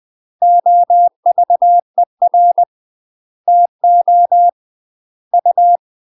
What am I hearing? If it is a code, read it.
Morse code, heard as a keyed tone, about 20 wpm, 700 Hz. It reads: OVER TO U
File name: UK MASTER QSOs TRAINING / 20WPM / UK MASTER QSOs 0032